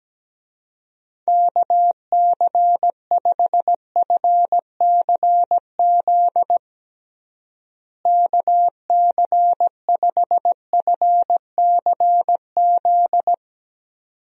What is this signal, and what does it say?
Morse code, heard as a keyed tone, about 17 wpm, 700 Hz. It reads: KC5FCZ KC5FCZ
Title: USA Call Signs - 6 Characters - Repeated 2x - 17wpm